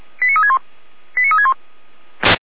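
Five-tone selective call (ZVEI), heard as quick runs of five tones, tones of about 70 ms. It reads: 89361 89361
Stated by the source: ZVEI five-tone alert received by SDR.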